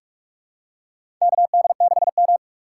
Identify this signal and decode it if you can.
Morse code, heard as a keyed tone, about 43 wpm, 700 Hz. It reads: KD6M
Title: USA Call Signs - 4 Characters - 3x Character Spacing - 45wpm